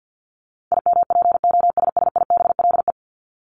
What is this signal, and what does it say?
Morse code, heard as a keyed tone, about 50 wpm, 700 Hz. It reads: IMPOSSIBLE
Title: Top 1001-1250 Words 50wpm